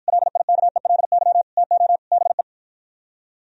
Morse code, heard as a keyed tone, about 52 wpm, 700 Hz. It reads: LIKELY TO BE